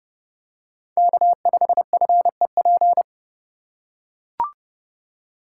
Morse code, heard as a keyed tone, about 30 wpm, 700 Hz. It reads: K5FEP